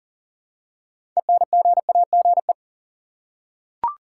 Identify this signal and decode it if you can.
Morse code, heard as a keyed tone, about 40 wpm, 700 Hz. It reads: ENGAGE
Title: Six Letter Words - 40wpm